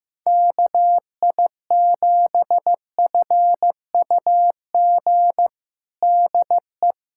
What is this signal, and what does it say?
Morse code, heard as a keyed tone, about 15 wpm, 700 Hz. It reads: KI7FUG DE
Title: USA Callsign DE Callsign - 6 Characters - 15wpm